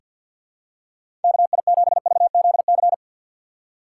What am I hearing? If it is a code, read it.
Morse code, heard as a keyed tone, about 49 wpm, 700 Hz. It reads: KI7VZC
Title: USA Call Signs - 6 Characters - 6x Character Spacing - 50wpm